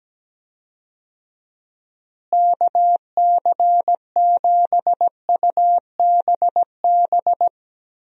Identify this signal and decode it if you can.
Morse code, heard as a keyed tone, about 17 wpm, 700 Hz. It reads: KC7UBB
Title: USA Call Signs - 6 Characters - Repeated 2x - 17wpm